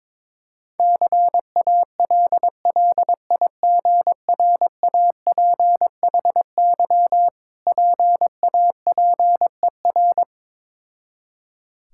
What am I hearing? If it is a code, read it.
Morse code, heard as a keyed tone, about 23 wpm, 690 Hz. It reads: CALLIGRAPHY PAPER
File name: Difficult Words with Hints - 22wpm